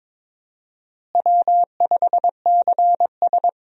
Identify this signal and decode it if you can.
Morse code, heard as a keyed tone, about 22 wpm, 700 Hz. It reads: W5CS
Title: USA Call Signs - 4 Characters - Repeated 2x - 22wpm